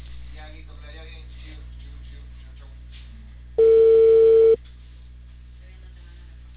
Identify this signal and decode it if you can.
Ringback tone (China, 450 Hz), with speech.